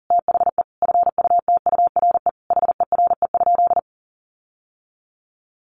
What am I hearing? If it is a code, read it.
Morse code, heard as a keyed tone, about 40 wpm, 700 Hz. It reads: THE FUTURE HERE?
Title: Sentences from Top 1500 Words 40wpm